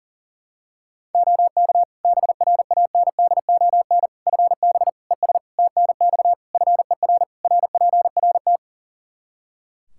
Morse code, heard as a keyed tone, about 37 wpm, 700 Hz. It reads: OK BRANDON FB ES TNX FER RPRT
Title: K1USN SST QSO Practice 40wpm